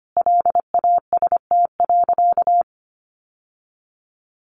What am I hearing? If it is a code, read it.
Morse code, heard as a keyed tone, about 25 wpm, 700 Hz. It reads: LAST.
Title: Sentences from Top 1100 Words 25wpm